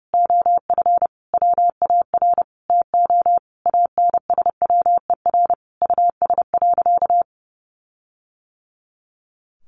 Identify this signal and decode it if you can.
Morse code, heard as a keyed tone, about 30 wpm, 700 Hz. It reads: OF WAR TO ANSWER US.
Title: Sentences from Top 1500 Words 30wpm